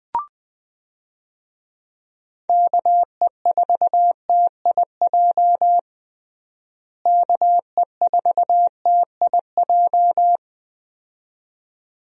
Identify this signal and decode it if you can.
Morse code, heard as a keyed tone, about 21 wpm, 700 Hz. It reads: KE4TIJ KE4TIJ
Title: USA Call Signs - 6 Characters - Repeated 2x - 20wpm